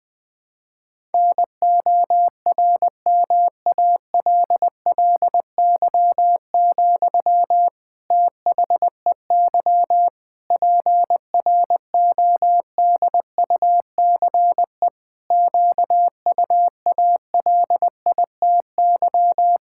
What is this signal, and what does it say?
Morse code, heard as a keyed tone, about 20 wpm, 700 Hz. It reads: NORMALLY, THEY PRODUCE QUALITY
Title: Sentences from Top 1000 Words - Review Part 6 - 20wpm